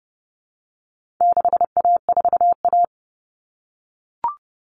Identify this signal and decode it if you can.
Morse code, heard as a keyed tone, about 30 wpm, 700 Hz. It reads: 6A4A
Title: Common Cut Numbers - 4-Digit 30wpm